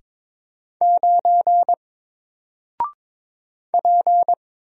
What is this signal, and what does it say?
Morse code, heard as a keyed tone, about 22 wpm, 700 Hz. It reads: 9 P
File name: Single-Letter-Number-22wpm